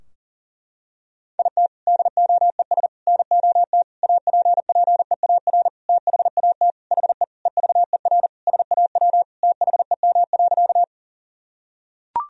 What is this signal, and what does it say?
Morse code, heard as a keyed tone, about 37 wpm, 700 Hz. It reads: IT DOES NOT APPEAR THAT HE EVER SAW THEM.
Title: Sentences from Top 1000 Words - Review Part 2 - 40wpm